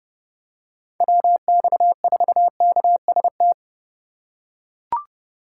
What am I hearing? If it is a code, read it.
Morse code, heard as a keyed tone, about 28 wpm, 700 Hz. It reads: WX4KST